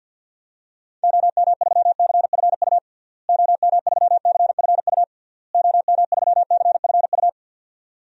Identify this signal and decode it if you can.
Morse code, heard as a keyed tone, about 52 wpm, 700 Hz. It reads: OM3CFU OM3CFU OM3CFU
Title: DXCC-100 Call Signs - 6 Characters - Repeated 3x - 50wpm